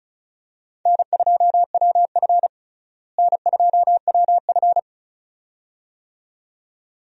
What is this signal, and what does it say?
Morse code, heard as a keyed tone, about 35 wpm, 700 Hz. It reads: N2WF N2WF
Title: USA Call Signs - 4 Characters - Repeated 2x - 35wpm